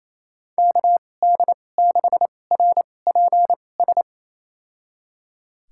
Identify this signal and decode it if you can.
Morse code, heard as a keyed tone, about 28 wpm, spaced as for 20 wpm, 700 Hz. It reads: KD6RPS
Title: USA Call Signs - 6 Characters - 2x Character Spacing - 28wpm